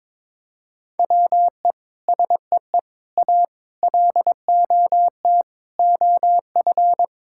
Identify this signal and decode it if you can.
Morse code, heard as a keyed tone, about 21 wpm, 700 Hz. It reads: WE SEE A LOT OF
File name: Sets of 9 Words - 22wpm